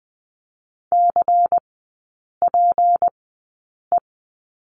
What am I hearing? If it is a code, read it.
Morse code, heard as a keyed tone, about 20 wpm, spaced as for 8 wpm, 700 Hz. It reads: CPE
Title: 152 Lesson 25e - Letter p - Three Characters 20wpm